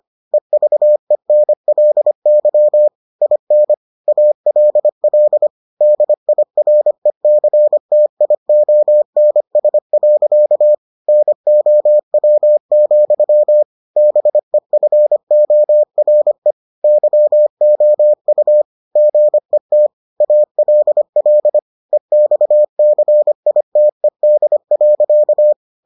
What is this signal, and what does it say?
Morse code, heard as a keyed tone, about 25 wpm, 600 Hz. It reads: EVENLY IN ALL DIRECTIONS. NOW, BEFORE YOU GET ALL EXCITED.